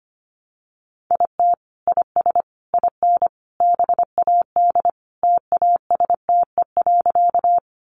Morse code, heard as a keyed tone, about 25 wpm, 700 Hz. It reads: IT IS IN BAD TASTE.
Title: Sentences from Top 1100 Words 25wpm